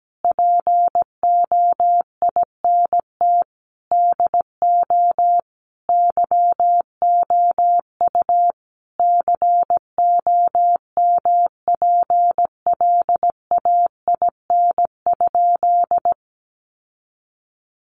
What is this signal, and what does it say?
Morse code, heard as a keyed tone, about 17 wpm, 700 Hz. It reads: POINT DO YOU COMPLAIN?
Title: Sentences from Top 1300 Words 17wpm